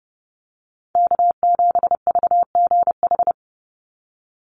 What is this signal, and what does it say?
Morse code, heard as a keyed tone, about 30 wpm, 700 Hz. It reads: K7VGH
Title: USA Call Signs 30wpm